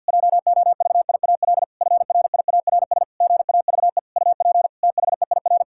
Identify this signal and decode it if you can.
Morse code, heard as a keyed tone, about 51 wpm, 700 Hz. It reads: JOVIAL FRIARS GAVE UP THEIR
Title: Pangram Challenge v2 - 50wpm